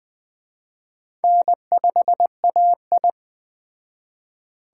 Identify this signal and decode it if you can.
Morse code, heard as a keyed tone, about 20 wpm, 700 Hz. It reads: N5AI
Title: Speed Racing - USA 1x2 Call Signs 17wpm